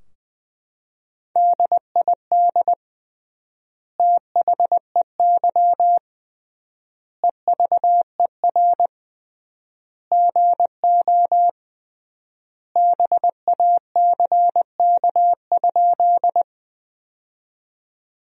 Morse code, heard as a keyed tone, about 20 wpm, 700 Hz. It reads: DID THEY EVER GO BACK?
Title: 3x Word-Spacing - Sentences from Top 400 Words - 20wpm